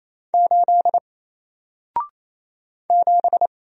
Morse code, heard as a keyed tone, about 27 wpm, 700 Hz. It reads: 8 7
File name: Single-Letter-Number-28wpm